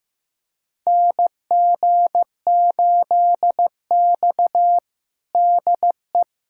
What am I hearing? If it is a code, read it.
Morse code, heard as a keyed tone, about 16 wpm, 700 Hz. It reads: NG8X DE
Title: USA Callsign DE Callsign - 4 Characters - 15wpm